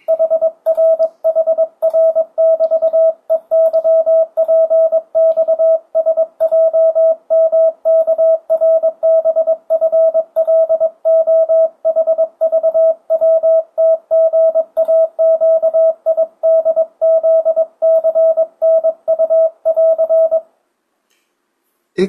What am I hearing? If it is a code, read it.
Morse code, heard as a keyed tone, about 22 wpm, 650 Hz. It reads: HRHR<BT>EYPXSJMKRBFLOHVWTGAQIDZCNU<AR>
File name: 2025cw / speedup / 22wpm